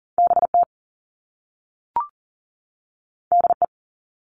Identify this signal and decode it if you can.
Morse code, heard as a keyed tone, about 41 wpm, 700 Hz. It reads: BT DE
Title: All Cut Numbers - 2-Digit 40wpm